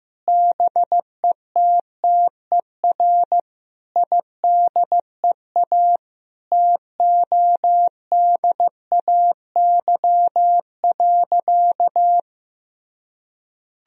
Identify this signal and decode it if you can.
Morse code, heard as a keyed tone, about 15 wpm, 700 Hz. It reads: BETTER IDEA TODAY.
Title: Sentences from Top 1000 Words - Review Part 4 - 15wpm